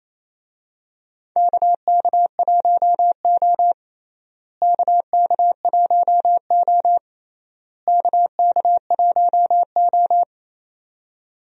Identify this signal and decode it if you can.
Morse code, heard as a keyed tone, about 28 wpm, 700 Hz. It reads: KK1O KK1O KK1O
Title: USA Call Signs - 4 Characters - Repeated 3x - 28wpm